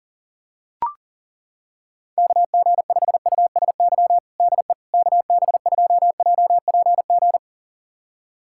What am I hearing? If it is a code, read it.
Morse code, heard as a keyed tone, about 40 wpm, 700 Hz. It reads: KG5USY DE KB2JPG
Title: USA Callsign DE Callsign - 6 Characters - 40wpm